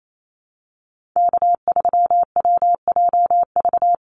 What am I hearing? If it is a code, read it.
Morse code, heard as a keyed tone, about 28 wpm, 700 Hz. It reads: K3WJV
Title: Call Signs Repeated 3x - 28wpm